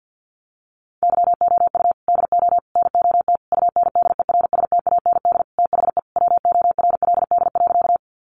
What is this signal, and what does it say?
Morse code, heard as a keyed tone, about 50 wpm, 700 Hz. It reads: YOU DO NOT UNDERSTAND THE WORLD.
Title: Sentences from Top 600 Words 50wpm